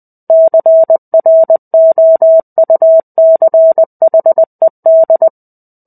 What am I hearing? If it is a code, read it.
Morse code, heard as a keyed tone, about 20 wpm, 650 Hz. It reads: CROUCHED
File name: 05-Chapter-05 20 WPM